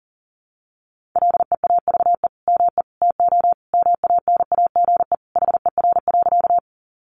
Morse code, heard as a keyed tone, about 40 wpm, 700 Hz. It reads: LEAVE ME TO MANAGE HER.
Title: Sentences from Top 1400 Words 40wpm